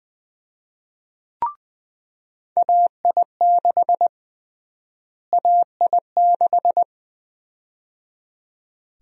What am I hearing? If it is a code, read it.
Morse code, heard as a keyed tone, about 20 wpm, 700 Hz. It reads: AI6 AI6
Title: USA Call Sign Prefixes - Repeated 2x - 20wpm